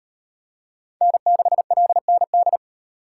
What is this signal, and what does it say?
Morse code, heard as a keyed tone, about 37 wpm, 700 Hz. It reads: N6LND